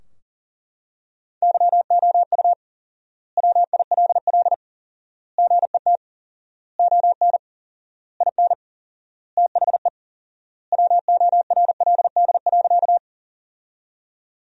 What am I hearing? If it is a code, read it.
Morse code, heard as a keyed tone, about 40 wpm, 700 Hz. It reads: YOU WILL GET ON IN THE WORLD.
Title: 4x Word-Spacing - Sentences from Top 200 Words - 40wpm